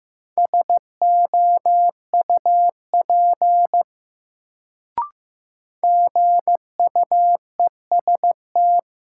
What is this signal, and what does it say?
Morse code, heard as a keyed tone, about 15 wpm, 700 Hz. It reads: SOUP GUEST
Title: Sentences from Top 1600 Words - New Word Drill Alt - 15wpm